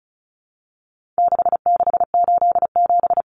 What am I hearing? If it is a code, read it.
Morse code, heard as a keyed tone, about 34 wpm, 700 Hz. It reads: 6687
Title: Common Cut Numbers - 4-Digit 35wpm